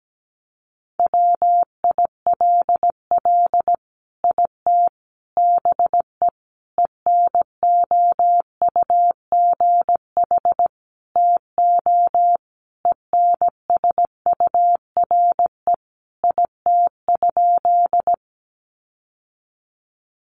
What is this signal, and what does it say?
Morse code, heard as a keyed tone, about 17 wpm, 700 Hz. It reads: WILL IT BE ENOUGH TO ENSURE IT?
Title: Sentences from Top 1100 Words 17wpm